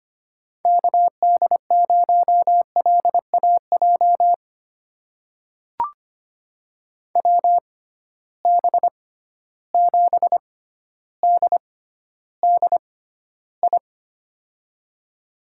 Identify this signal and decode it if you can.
Morse code, heard as a keyed tone, about 25 wpm, spaced as for 13 wpm, 700 Hz. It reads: KD0LAJ WB7DDI
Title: USA Call Signs - 6 Characters - 6x Character Spacing - 25wpm